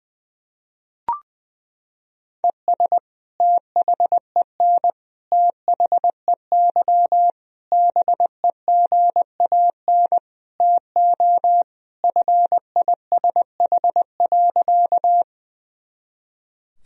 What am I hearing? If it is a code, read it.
Morse code, heard as a keyed tone, about 20 wpm, 700 Hz. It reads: ES THEN THEY BEGAN TO FISH.
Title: Sentences from Top 300 Words and Abbreviations 20wpm